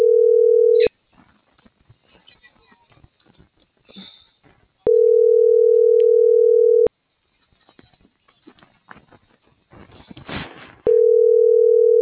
Ringback tone (North America), 440 + 480 Hz.
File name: CENTRAL TERRESTRE / out vane.lazaro